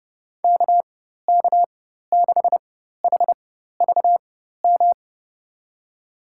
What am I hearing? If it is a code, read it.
Morse code, heard as a keyed tone, about 30 wpm, spaced as for 14 wpm, 700 Hz. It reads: KK6HVM
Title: USA Call Signs - 6 Characters - 4x Character Spacing - 30wpm